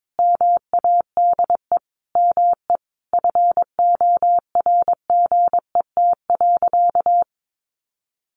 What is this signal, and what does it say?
Morse code, heard as a keyed tone, about 22 wpm, 700 Hz. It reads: MADE ME FORGET.
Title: Sentences from Top 1500 Words 22wpm